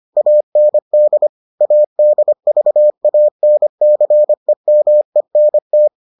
Morse code, heard as a keyed tone, about 25 wpm, 600 Hz. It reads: AND ADVANCEMENT